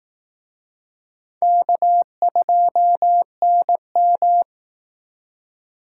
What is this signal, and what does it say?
Morse code, heard as a keyed tone, about 18 wpm, 700 Hz. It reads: K2NM
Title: Speed Racing - USA 1x2 Call Signs 15wpm